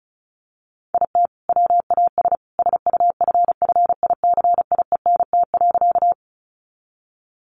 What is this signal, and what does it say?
Morse code, heard as a keyed tone, about 34 wpm, 700 Hz. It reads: IT WAS SUFFICIENT.
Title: Sentences from Top 1400 Words 35wpm